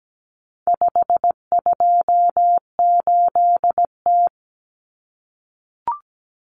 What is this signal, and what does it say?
Morse code, heard as a keyed tone, about 17 wpm, 700 Hz. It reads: 528T
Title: Common Cut Numbers - 4-Digit 17wpm